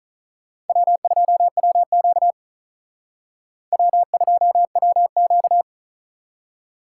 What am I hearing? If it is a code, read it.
Morse code, heard as a keyed tone, about 43 wpm, 700 Hz. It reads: W2WQ W2WQ
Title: Speed Racing - USA 1x2 Call Signs 35wpm